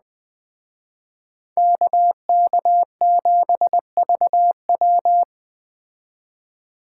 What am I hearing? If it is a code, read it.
Morse code, heard as a keyed tone, about 20 wpm, 700 Hz. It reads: KK7VW
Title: Speed Racing - USA 2x2 Call Signs 17wpm